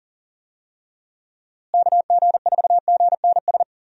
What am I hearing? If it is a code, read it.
Morse code, heard as a keyed tone, about 40 wpm, 700 Hz. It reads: KG4GNS
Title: USA Call Signs - 6 Characters - Repeated 2x - 40wpm